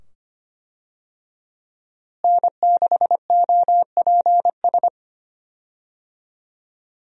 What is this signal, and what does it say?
Morse code, heard as a keyed tone, about 25 wpm, 700 Hz. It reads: N6OPS
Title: USA Call Signs - 5 Characters - Repeated 2x - 25wpm